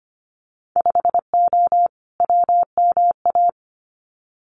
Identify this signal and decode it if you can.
Morse code, heard as a keyed tone, about 25 wpm, 700 Hz. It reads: 5O WMA
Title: Winter Field Day Exchanges Encore 25wpm